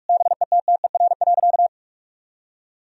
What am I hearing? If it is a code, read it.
Morse code, heard as a keyed tone, about 46 wpm, 700 Hz. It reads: BETTER.
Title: Sentences from Top 1000 Words - Review Part 4 - 45wpm